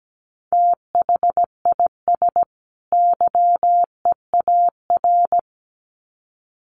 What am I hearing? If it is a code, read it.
Morse code, heard as a keyed tone, about 17 wpm, 700 Hz. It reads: THIS YEAR
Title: Sets of 2 Words 17wpm